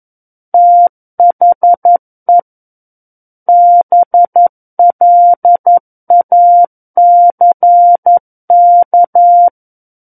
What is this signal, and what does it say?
Morse code, heard as a keyed tone, about 11 wpm, 700 Hz. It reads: THE BLACK
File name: Book One / 02-Chapter 2 - 11 WPM